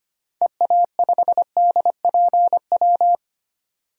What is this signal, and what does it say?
Morse code, heard as a keyed tone, about 25 wpm, 700 Hz. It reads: EA5DPW
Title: DXCC-100 Call Signs - 6 Characters - 25wpm